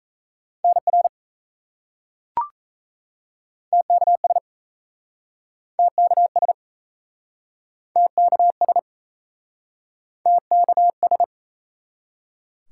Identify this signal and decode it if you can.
Morse code, heard as a keyed tone, about 38 wpm, 700 Hz. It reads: NR TKS TKS TKS TKS